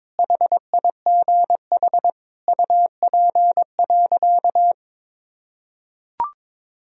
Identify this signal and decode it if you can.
Morse code, heard as a keyed tone, about 22 wpm, 700 Hz. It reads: HIGH UP.
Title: Sentences from Top 1000 Words - Review Part 1 - 22wpm